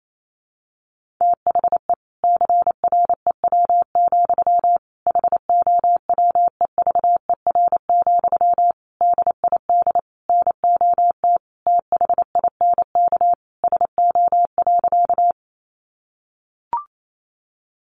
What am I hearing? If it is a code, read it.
Morse code, heard as a keyed tone, about 28 wpm, 700 Hz. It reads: THE CREW, HOWEVER, DID NOT THINK SO.
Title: Sentences from Top 1500 Words 28wpm